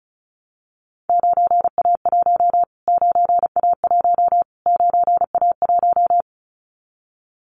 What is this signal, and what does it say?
Morse code, heard as a keyed tone, about 34 wpm, 700 Hz. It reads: 9A1 9A1 9A1
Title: Callsign Prefixes for Top 25 DXCC - Repeated 3x - 35wpm